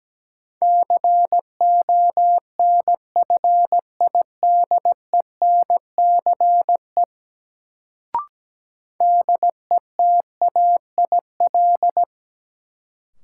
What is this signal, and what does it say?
Morse code, heard as a keyed tone, about 17 wpm, 700 Hz. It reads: CONFIDENCE DETAIL
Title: Sentences from Top 1500 Words - New Word Drill Alt - 17wpm